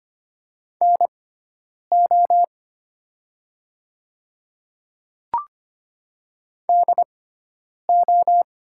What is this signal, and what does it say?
Morse code, heard as a keyed tone, about 25 wpm, spaced as for 9 wpm, 700 Hz. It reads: NO DO
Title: 6x Character Spacing - Top 300 Words - 25wpm